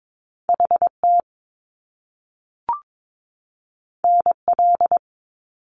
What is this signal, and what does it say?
Morse code, heard as a keyed tone, about 22 wpm, 700 Hz. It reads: HT NL
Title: Top Two Letter Combinations in Words 22wpm